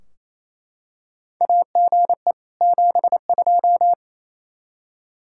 Morse code, heard as a keyed tone, about 28 wpm, 700 Hz. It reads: AGE 72
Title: Practice QSO - Exchange 3 - More Info - 28wpm